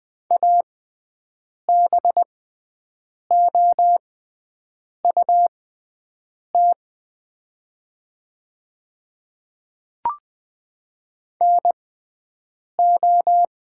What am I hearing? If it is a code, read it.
Morse code, heard as a keyed tone, about 20 wpm, spaced as for 7 wpm, 700 Hz. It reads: ABOUT NO
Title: 6x Character Spacing - Top 100 Words - 20wpm